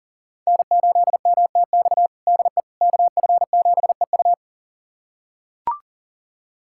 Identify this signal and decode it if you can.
Morse code, heard as a keyed tone, about 40 wpm, 700 Hz. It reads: N8MTX DE KF7EU
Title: USA Callsign DE Callsign - 5 Characters - 40wpm